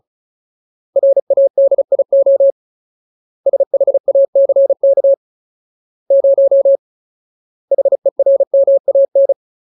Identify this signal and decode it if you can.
Morse code, heard as a keyed tone, about 37 wpm, 550 Hz. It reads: RADIO SHACK 0 HERMAN
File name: Ultimate Word List / 35WPM Ultimate